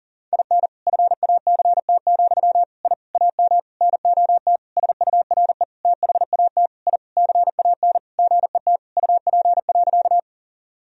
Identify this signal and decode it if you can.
Morse code, heard as a keyed tone, about 40 wpm, 700 Hz. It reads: IN FACT, I AM NOT SURE THAT I CAN GET UP.
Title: Sentences from Top 1000 Words - Review Part 3 - 40wpm